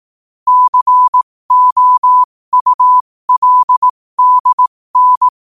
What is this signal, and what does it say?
Morse code, heard as a keyed tone, about 18 wpm, spaced as for 15 wpm, 1000 Hz. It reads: COULDN
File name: qotdfiles / 15wpm